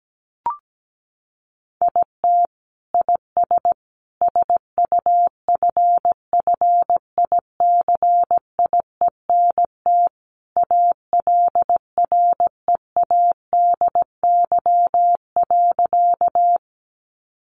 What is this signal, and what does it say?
Morse code, heard as a keyed tone, about 17 wpm, 700 Hz. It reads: IT IS SUFFICIENT ALREADY.
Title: Sentences from Top 1400 Words 17wpm